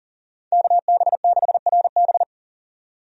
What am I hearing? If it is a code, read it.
Morse code, heard as a keyed tone, about 40 wpm, 700 Hz. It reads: KB6RB
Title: USA Call Signs - 5 Characters - 40wpm